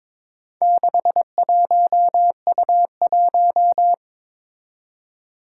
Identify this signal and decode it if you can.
Morse code, heard as a keyed tone, about 23 wpm, 700 Hz. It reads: 61U1
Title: Four Challenging Letter-Number Combinations 22wpm